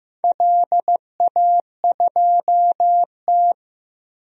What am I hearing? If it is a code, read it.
Morse code, heard as a keyed tone, about 15 wpm, 700 Hz. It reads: LA2T
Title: DXCC-100 Call Signs - 4 Characters - 15wpm